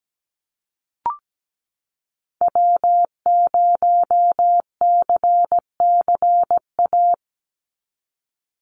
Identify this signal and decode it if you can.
Morse code, heard as a keyed tone, about 17 wpm, 700 Hz. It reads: W0CCA
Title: NA SOTA Activator Callsigns 17wpm